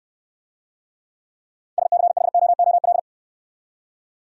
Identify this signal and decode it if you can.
Morse code, heard as a keyed tone, about 68 wpm, 700 Hz. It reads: SCHOOL